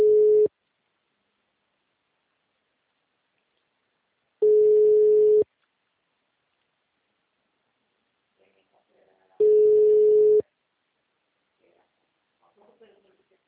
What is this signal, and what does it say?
Ringback tone (Europe, 425 Hz), with speech.